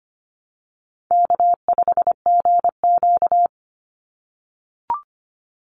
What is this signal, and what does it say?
Morse code, heard as a keyed tone, about 25 wpm, 700 Hz. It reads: K5GQ
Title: Easy Call Signs - 25wpm